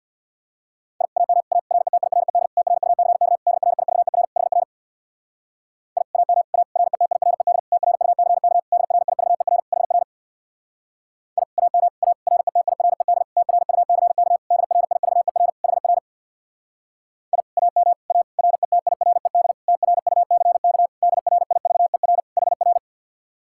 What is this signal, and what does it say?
Morse code, heard as a keyed tone, about 62 wpm, 700 Hz. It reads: I AM A RETIRED TRUCK DRIVER HR I AM A RETIRED TRUCK DRIVER HR I AM A RETIRED TRUCK DRIVER HR I AM A RETIRED TRUCK DRIVER HR
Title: Speed-Racing Common QSO Phrases - 50wpm